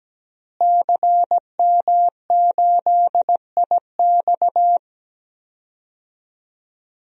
Morse code, heard as a keyed tone, about 17 wpm, 700 Hz. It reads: CM8IX
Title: DXCC-100 Call Signs - 5 Characters - Repeated 2x - 17wpm